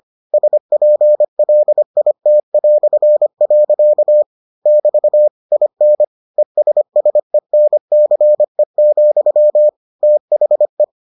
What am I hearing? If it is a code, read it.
Morse code, heard as a keyed tone, about 25 wpm, 600 Hz. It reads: SPLIT". <BT> IN ESSENCE, THE